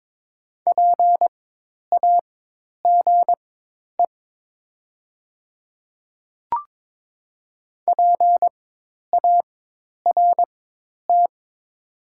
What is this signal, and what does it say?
Morse code, heard as a keyed tone, about 23 wpm, spaced as for 10 wpm, 700 Hz. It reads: PAGE PART
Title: 4x Character Spacing - Top 300 Words - 22wpm